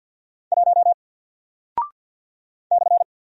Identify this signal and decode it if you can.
Morse code, heard as a keyed tone, about 48 wpm, 700 Hz. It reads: 1 /
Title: Single Character (Letter:Number:Punctuation:Prosign) Alt 50wpm